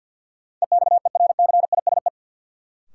Morse code, heard as a keyed tone, about 52 wpm, 700 Hz. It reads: EXERCISE
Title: Sentences from Top 1000 Words - Review Part 4 - New Word Drill - 50wpm